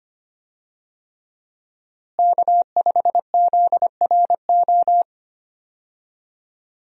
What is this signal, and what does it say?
Morse code, heard as a keyed tone, about 25 wpm, 700 Hz. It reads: K5ZRO
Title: USA Call Signs - 5 Characters - Repeated 3x - 25wpm